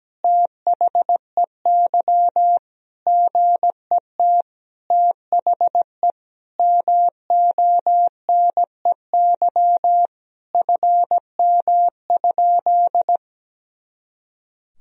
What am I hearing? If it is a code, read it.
Morse code, heard as a keyed tone, about 17 wpm, 700 Hz. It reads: THEY GET THE MONEY FM?
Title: Sentences from Top 400 Words and Abbreviations 17wpm